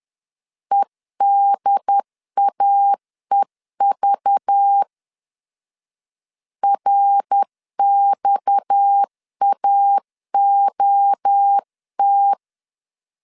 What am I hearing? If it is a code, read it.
Morse code, heard as a keyed tone, about 10 wpm, 800 Hz. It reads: EDAEV RXAOT
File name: CW / DET / 1 6 1500C S C 10wpm 4252